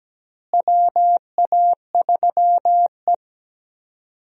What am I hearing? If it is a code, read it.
Morse code, heard as a keyed tone, about 17 wpm, 700 Hz. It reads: WA3E
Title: USA Call Signs - 4 Characters - Repeated 2x - 17wpm